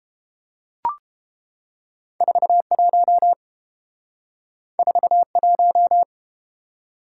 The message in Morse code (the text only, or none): 41 41